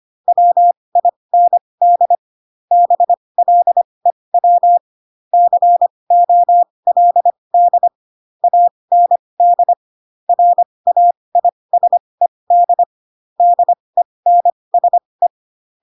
Morse code, heard as a keyed tone, about 25 wpm, spaced as for 20 wpm, 700 Hz. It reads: WIND BLEW COLD AND RAISED DENSE
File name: Pinocchio / Pinocchio ch6 15 25 ik1hge